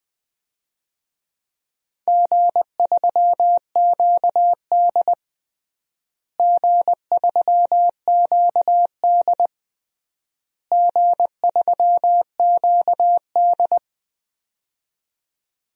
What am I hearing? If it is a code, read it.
Morse code, heard as a keyed tone, about 20 wpm, 700 Hz. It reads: G3QD G3QD G3QD
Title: DXCC-100 Call Signs - 4 Characters - Repeated 3x - 20wpm